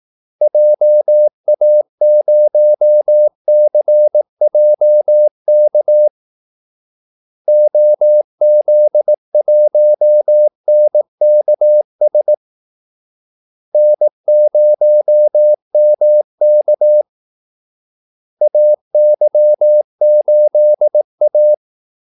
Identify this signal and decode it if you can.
Morse code, heard as a keyed tone, about 18 wpm, 600 Hz. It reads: JA0CJK OZ1NKS N0MK AY8A
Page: Kaldesignaler 18 wpm